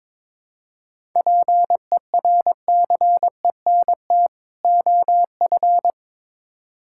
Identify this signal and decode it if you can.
Morse code, heard as a keyed tone, about 22 wpm, 700 Hz. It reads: PERCENT OF
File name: Build A Phrase - 2-Word Phrases - 22wpm